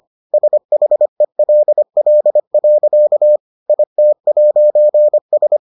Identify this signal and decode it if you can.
Morse code, heard as a keyed tone, about 25 wpm, 600 Hz. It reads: SHELL. IT'S